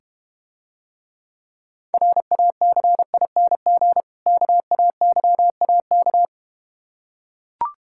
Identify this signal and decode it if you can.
Morse code, heard as a keyed tone, about 32 wpm, 700 Hz. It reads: RACING KAYAK